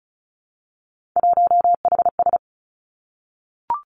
Morse code, heard as a keyed tone, about 36 wpm, 700 Hz. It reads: 1HS
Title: Three Challenging Letter-Number Combinations 35wpm